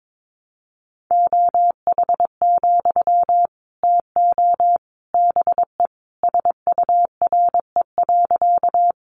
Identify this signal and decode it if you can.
Morse code, heard as a keyed tone, about 22 wpm, 700 Hz. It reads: OH, TO BE SURE.
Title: Sentences from Top 500 Words 22wpm